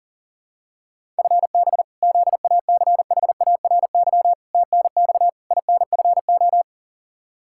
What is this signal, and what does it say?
Morse code, heard as a keyed tone, about 40 wpm, 700 Hz. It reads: FB ZACHARY TNX INFO
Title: Practice QSO - Exchange 3 - More Info - 40wpm